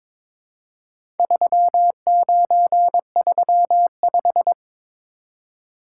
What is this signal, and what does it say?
Morse code, heard as a keyed tone, about 22 wpm, 700 Hz. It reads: 3935
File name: practice / 22wpm / Four Letter-Number Combinations - 22wpm